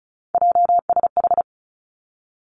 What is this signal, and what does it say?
Morse code, heard as a keyed tone, about 36 wpm, 700 Hz. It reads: JSH
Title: Three Challenging Letter-Number Combinations 35wpm